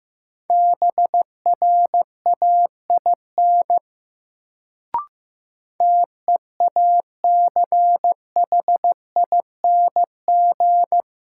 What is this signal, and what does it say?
Morse code, heard as a keyed tone, about 15 wpm, 700 Hz. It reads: BRAIN TEACHING
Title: Sentences from Top 1100 Words - New Word Drill Alt - 15wpm